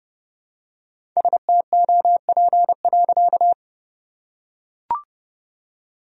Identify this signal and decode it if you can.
Morse code, heard as a keyed tone, about 30 wpm, 700 Hz. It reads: STOP.
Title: Sentences from Top 1000 Words - Review Part 1 - 30wpm